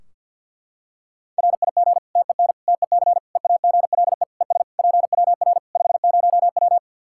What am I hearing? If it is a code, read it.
Morse code, heard as a keyed tone, about 46 wpm, 700 Hz. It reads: RIG TEN TEC EAGLE ES PWR 50W
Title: Practice QSO - Exchange 2 - Info - 50wpm